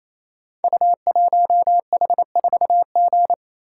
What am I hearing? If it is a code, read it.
Morse code, heard as a keyed tone, about 28 wpm, 700 Hz. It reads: U1H4G
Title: Code Groups - 5 Letter:Numbers - 28wpm